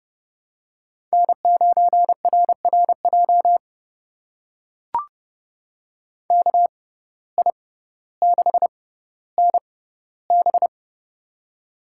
Morse code, heard as a keyed tone, about 32 wpm, spaced as for 16 wpm, 700 Hz. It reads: N9RRJ KI6NB
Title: USA Call Signs - 5 Characters - 6x Character Spacing - 30wpm